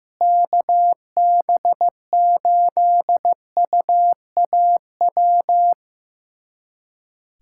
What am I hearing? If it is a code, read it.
Morse code, heard as a keyed tone, about 15 wpm, 700 Hz. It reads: KB8UAW
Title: USA Callsign DE Callsign 15wpm